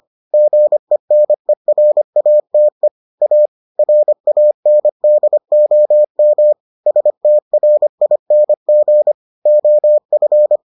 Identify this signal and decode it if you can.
Morse code, heard as a keyed tone, about 25 wpm, 600 Hz. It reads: GENERATE A RANDOM STRING OF